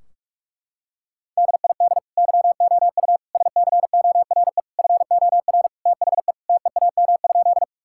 Morse code, heard as a keyed tone, about 46 wpm, 700 Hz. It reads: DID YOU SCORE FOR THE TEAM?
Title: Sentences from Top 1600 Words 45wpm